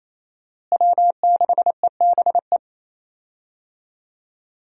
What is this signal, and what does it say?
Morse code, heard as a keyed tone, about 27 wpm, 700 Hz. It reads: W6EBE
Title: USA Call Signs - 5 Characters - Repeated 2x - 28wpm